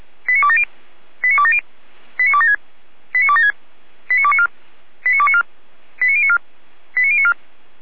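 Five-tone selective call (ZVEI), heard as quick runs of five tones, tones of about 70 ms. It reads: 89270 89270 89276 89276 89294 89294 89094 89094
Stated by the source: ZVEI five-tone selective call received on SDR, from the fire brigade (Feuerwehr) Orlamünde audio server.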